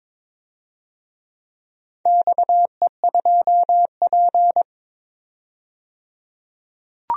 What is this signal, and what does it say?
Morse code, heard as a keyed tone, about 22 wpm, 700 Hz. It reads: XE2P
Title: DXCC-100 Call Signs - 4 Characters - 22wpm